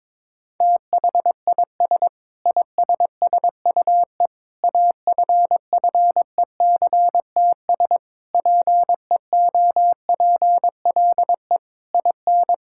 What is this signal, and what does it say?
Morse code, heard as a keyed tone, about 22 wpm, 700 Hz. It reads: THIS ISSUE AFFECTS PEOPLE IN
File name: practice / 22wpm / Sentences with 8-Word Phrases - 22wpm